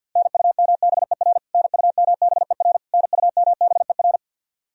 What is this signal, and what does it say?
Morse code, heard as a keyed tone, about 53 wpm, 700 Hz. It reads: NUMBER NUMBER NUMBER
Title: Top 100 Words Repeated 50wpm